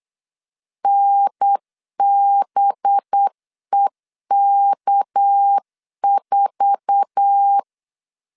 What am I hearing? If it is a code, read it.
Morse code, heard as a keyed tone, about 8 wpm, 800 Hz. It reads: NBEK4
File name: LEZIONI / 8 YJ45 / 8 YJ4 8wpm 3246